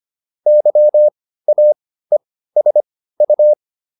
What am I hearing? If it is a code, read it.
Morse code, heard as a keyed tone, about 25 wpm, spaced as for 15 wpm, 600 Hz. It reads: YAESU
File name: QSO401 15